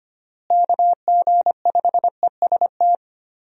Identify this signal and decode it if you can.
Morse code, heard as a keyed tone, about 25 wpm, 700 Hz. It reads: KG5EST
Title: USA Call Signs - 6 Characters - Repeated 3x - 25wpm